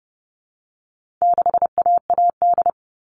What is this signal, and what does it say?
Morse code, heard as a keyed tone, about 30 wpm, 700 Hz. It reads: 6AAD
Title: All Cut Numbers - 4-Digit 30wpm